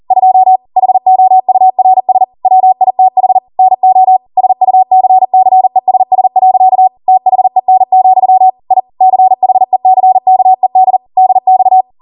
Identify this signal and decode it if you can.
Morse code, heard as a keyed tone, about 40 wpm, 750 Hz. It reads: 2 HOURS WITH NO SUCCESS. THEN, I CHECKED DX